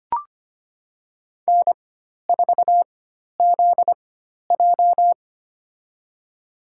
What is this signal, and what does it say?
Morse code, heard as a keyed tone, about 25 wpm, spaced as for 12 wpm, 700 Hz. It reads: N4ZJ
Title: USA Call Signs - 4x Character Spacing - 25wpm